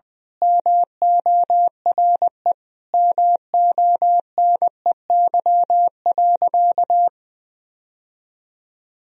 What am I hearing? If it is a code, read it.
Morse code, heard as a keyed tone, about 20 wpm, 700 Hz. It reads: MORE MONEY.
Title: Sentences from Top 1000 Words - Review Part 4 - 20wpm